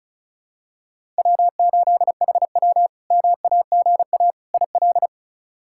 Morse code, heard as a keyed tone, about 36 wpm, 700 Hz. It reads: W8HW MAGA IL
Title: K1USN SST QSO Practice - 35wpm